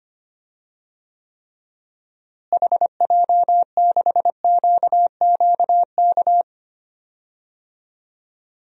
Morse code, heard as a keyed tone, about 25 wpm, 700 Hz. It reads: HJ6QQK
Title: DXCC-100 Call Signs - 6 Characters - Repeated 3x - 25wpm